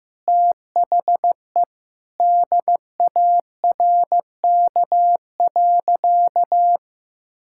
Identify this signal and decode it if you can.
Morse code, heard as a keyed tone, about 15 wpm, 700 Hz. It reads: THE DARK.
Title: Sentences from Top 1600 Words 15wpm